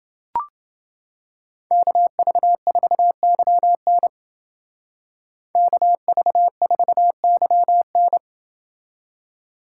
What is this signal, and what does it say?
Morse code, heard as a keyed tone, about 28 wpm, 700 Hz. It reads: KV4YN KV4YN